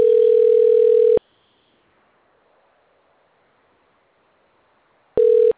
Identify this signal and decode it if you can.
Ringback tone (North America), 440 + 480 Hz.